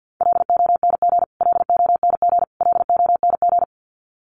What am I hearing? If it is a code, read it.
Morse code, heard as a keyed tone, about 50 wpm, 700 Hz. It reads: LONG LONG LONG
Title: Top 500 Words Repeated 50wpm